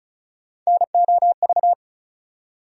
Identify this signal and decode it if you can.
Morse code, heard as a keyed tone, about 36 wpm, 700 Hz. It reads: NOV
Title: USA Call Sign Postfixes - 3 Letters 35wpm